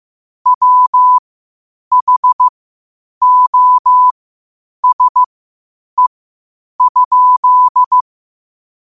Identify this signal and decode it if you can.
Morse code, heard as a keyed tone, about 15 wpm, spaced as for 8 wpm, 1000 Hz. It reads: WHOSE?
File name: qotdfiles / 8wpm